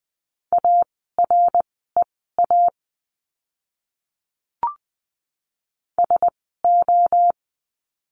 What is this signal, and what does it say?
Morse code, heard as a keyed tone, about 20 wpm, spaced as for 14 wpm, 700 Hz. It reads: AREA SO
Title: 061 Lesson 11f - Letter h - Words 20wpm